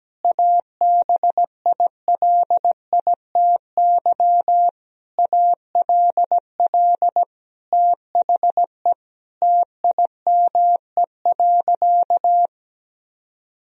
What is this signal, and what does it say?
Morse code, heard as a keyed tone, about 17 wpm, 700 Hz. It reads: ABILITY ALL THE TIME.
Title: Sentences from Top 1000 Words - Review Part 3 - 17wpm